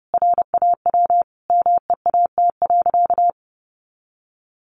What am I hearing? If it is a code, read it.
Morse code, heard as a keyed tone, about 30 wpm, 700 Hz. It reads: RAW MEAT.
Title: Sentences from Top 1500 Words 30wpm